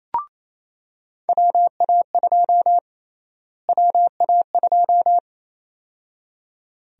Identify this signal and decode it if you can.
Morse code, heard as a keyed tone, about 28 wpm, 700 Hz. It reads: WA2 WA2
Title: USA Call Sign Prefixes - Repeated 2x - 28wpm